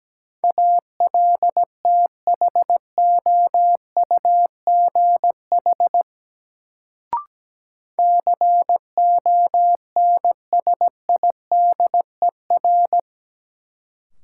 Morse code, heard as a keyed tone, about 17 wpm, 700 Hz. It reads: ALTHOUGH CONSIDER
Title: Sentences from Top 600 Words - New Word Drill - 17wpm